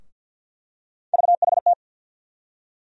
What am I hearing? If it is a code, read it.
Morse code, heard as a keyed tone, about 47 wpm, 700 Hz. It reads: VHT
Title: USA Call Sign Postfixes - 3 Letters 50wpm